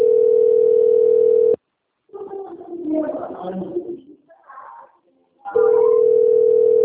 Ringback tone (North America), 440 + 480 Hz.